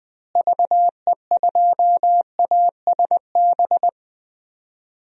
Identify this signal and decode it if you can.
Morse code, heard as a keyed tone, about 20 wpm, 700 Hz. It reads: VE2ASB
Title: DXCC-100 Call Signs - 6 Characters - Repeated 3x - 20wpm